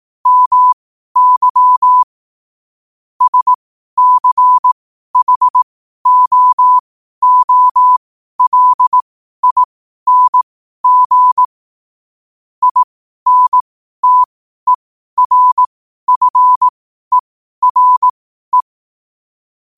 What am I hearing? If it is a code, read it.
Morse code, heard as a keyed tone, about 18 wpm, spaced as for 12 wpm, 1000 Hz. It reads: MY SCHOOLING INTERFERE
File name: qotdfiles / 12wpm